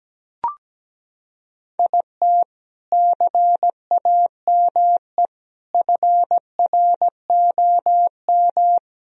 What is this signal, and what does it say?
Morse code, heard as a keyed tone, about 17 wpm, 700 Hz. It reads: IT CAME FROM
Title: Sentences from Top 1000 Words - Review Part 2 - 17wpm